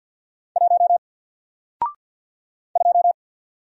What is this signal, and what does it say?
Morse code, heard as a keyed tone, about 48 wpm, 700 Hz. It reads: 1 2
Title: Single Letter-Number Alt 50wpm